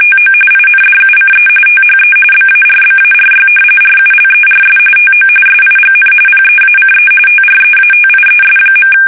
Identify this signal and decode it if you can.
RTTY, about 45.45 baud. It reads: TO INDUCE THE BARBARY RULERS TO SIGN PEACE TREATIES.
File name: fsk850 / usnavy1